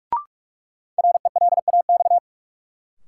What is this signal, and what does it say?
Morse code, heard as a keyed tone, about 46 wpm, 700 Hz. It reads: RELAX
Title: Sentences from Top 1400 Words - New Word Drill - 45wpm